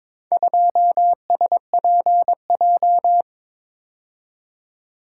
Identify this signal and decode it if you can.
Morse code, heard as a keyed tone, about 22 wpm, 700 Hz. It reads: 2SPJ
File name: practice / 22wpm / Four Letter-Number Combinations - 22wpm